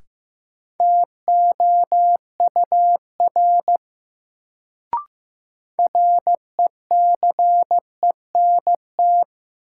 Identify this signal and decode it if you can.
Morse code, heard as a keyed tone, about 15 wpm, 700 Hz. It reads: TOUR RECENT
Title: Sentences from Top 1400 Words - New Word Drill Alt - 15wpm